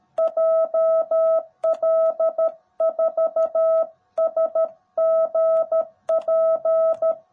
Morse code, heard as a keyed tone, about 13 wpm, 650 Hz. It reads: JL4SGP
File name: jh8cbh / 2021cw / cw18b